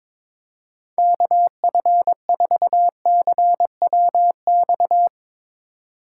Morse code, heard as a keyed tone, about 22 wpm, 700 Hz. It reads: KF4CWX
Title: Speed Racing - USA 2x3 Call Signs 22wpm